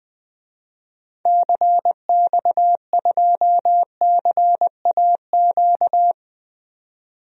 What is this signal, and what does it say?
Morse code, heard as a keyed tone, about 20 wpm, 700 Hz. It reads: CX2CAQ
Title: DXCC-100 Call Signs - 6 Characters - 20wpm